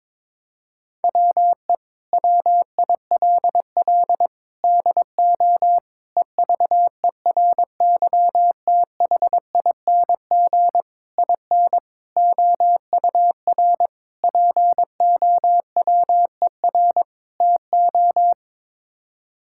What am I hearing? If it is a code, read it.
Morse code, heard as a keyed tone, about 22 wpm, 700 Hz. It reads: WE WILL DO EVERYTHING IN OUR POWER TO
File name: practice / 22wpm / Sets of 8 Words - 22wpm